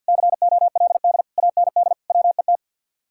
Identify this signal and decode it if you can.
Morse code, heard as a keyed tone, about 67 wpm, spaced as for 58 wpm, 700 Hz. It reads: COLD AND WET
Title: Speed-Racing Common QSO Phrases Encore - 50wpm